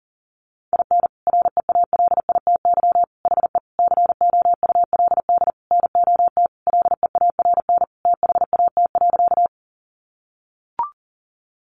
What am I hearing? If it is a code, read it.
Morse code, heard as a keyed tone, about 40 wpm, 700 Hz. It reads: IN REALITY HE COULD NOT LEARN THAT.
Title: Sentences from Top 1200 Words 40wpm